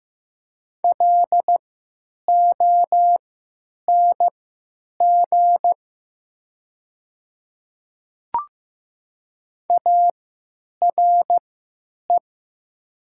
Morse code, heard as a keyed tone, about 15 wpm, spaced as for 9 wpm, 700 Hz. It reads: LONG ARE
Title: 3x Character Spacing - Top 200 Words - 15wpm